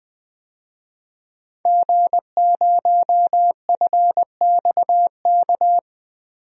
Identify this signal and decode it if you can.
Morse code, heard as a keyed tone, about 20 wpm, 700 Hz. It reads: G0FXK
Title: DXCC-100 Call Signs - 5 Characters - Repeated 2x - 20wpm